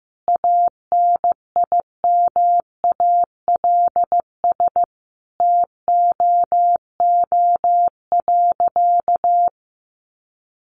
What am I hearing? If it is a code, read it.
Morse code, heard as a keyed tone, about 15 wpm, 700 Hz. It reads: ANIMALS TOO.
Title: Easy Phrases 15wpm